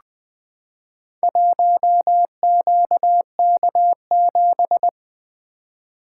Morse code, heard as a keyed tone, about 20 wpm, 700 Hz. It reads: 1QK7
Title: Four Letter-Number Combinations - 20wpm